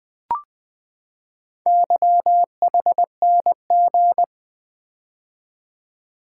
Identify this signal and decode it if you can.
Morse code, heard as a keyed tone, about 20 wpm, 700 Hz. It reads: YHNG
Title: Four Letter-Number Combinations - 20wpm